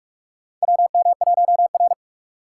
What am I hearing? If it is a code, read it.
Morse code, heard as a keyed tone, about 45 wpm, 690 Hz. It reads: WM1R
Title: USA Call Signs - 4 Characters - 2x Character Spacing - 45wpm